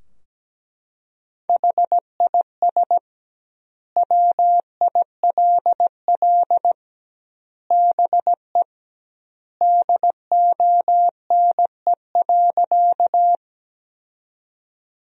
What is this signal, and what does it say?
Morse code, heard as a keyed tone, about 17 wpm, 700 Hz. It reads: HIS WILL BE DONE.
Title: 2x Word-Spacing - Sentences from Top 500 Words - 17wpm